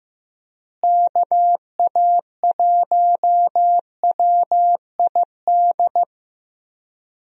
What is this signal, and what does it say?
Morse code, heard as a keyed tone, about 16 wpm, 700 Hz. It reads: KA1WID
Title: USA Call Signs - 6 Characters - Repeated 3x - 15wpm